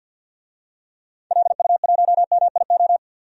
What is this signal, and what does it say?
Morse code, heard as a keyed tone, about 51 wpm, 700 Hz. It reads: LU1MIO
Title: DXCC-100 Call Signs - 6 Characters - 50wpm